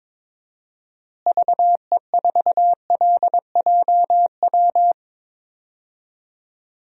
Morse code, heard as a keyed tone, about 21 wpm, 700 Hz. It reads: VE4LJW
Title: DXCC-100 Call Signs - 6 Characters - Repeated 2x - 22wpm